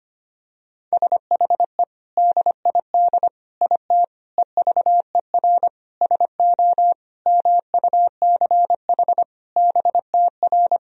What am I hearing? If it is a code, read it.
Morse code, heard as a keyed tone, about 25 wpm, 700 Hz. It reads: SHE DID IT EVER SO MUCH BTR
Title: Sentences from Top 400 Words and Abbreviations 25wpm